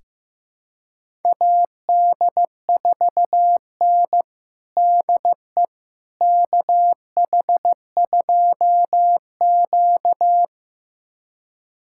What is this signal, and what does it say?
Morse code, heard as a keyed tone, about 15 wpm, 700 Hz. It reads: AD4N DE KH2Q
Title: USA Callsign DE Callsign - 4 Characters - 15wpm